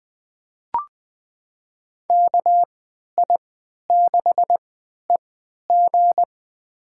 Morse code, heard as a keyed tone, about 20 wpm, spaced as for 11 wpm, 700 Hz. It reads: KI6EG
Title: USA Call Signs - 5 Characters - 3x Character Spacing - 20wpm